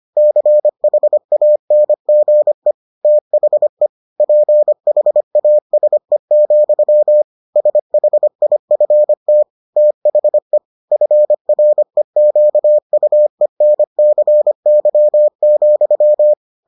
Morse code, heard as a keyed tone, about 25 wpm, 600 Hz. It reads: CHANGE THE PHASE, SHIFT THE FREQUENCY,